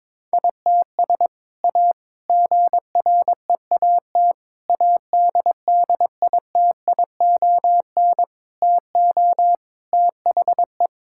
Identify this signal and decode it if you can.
Morse code, heard as a keyed tone, about 22 wpm, 700 Hz. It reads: ITS A GREAT ADDITION TO THE
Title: Sentences from Top 1000 Words - Review Part 6 - 22wpm